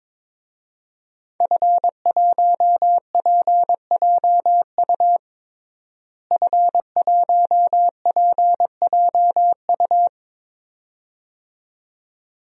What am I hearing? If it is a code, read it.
Morse code, heard as a keyed tone, about 22 wpm, 700 Hz. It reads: F1PJU F1PJU
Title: DXCC-100 Call Signs - 5 Characters - Repeated 2x - 22wpm